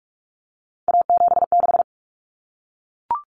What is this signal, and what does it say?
Morse code, heard as a keyed tone, about 44 wpm, 700 Hz. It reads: A76
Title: Common Cut Numbers - 3-Digit 45wpm